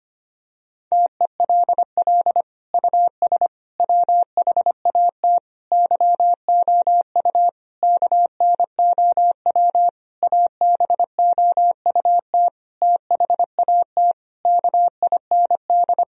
Morse code, heard as a keyed tone, about 25 wpm, 700 Hz. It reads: TELL US WHAT YOU KNOW ABOUT THAT KIND
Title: Sentences from Top 1000 Words - Review Part 4 - 25wpm